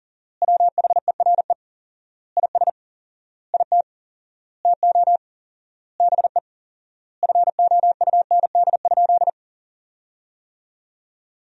Morse code, heard as a keyed tone, about 40 wpm, 700 Hz. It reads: WHERE IS IT TO BE FOUND?
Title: 4x Word-Spacing - Sentences from Top 300 Words - 40wpm